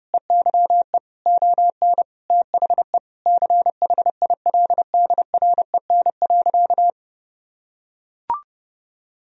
Morse code, heard as a keyed tone, about 30 wpm, 700 Hz. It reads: EYE ON THE CHILDREN.
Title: Sentences from Top 300 Words and Abbreviations 30wpm